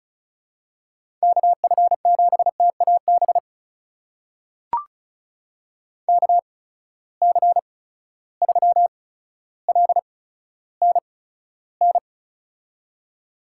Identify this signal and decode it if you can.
Morse code, heard as a keyed tone, about 36 wpm, spaced as for 15 wpm, 700 Hz. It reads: KF7TAB KC3LNN
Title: USA Call Signs - 6 Characters - 8x Character Spacing - 35wpm